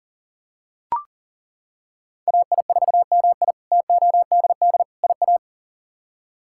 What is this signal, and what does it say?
Morse code, heard as a keyed tone, about 40 wpm, 700 Hz. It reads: AI4MI TODD IA
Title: K1USN SST QSO Practice - 40wpm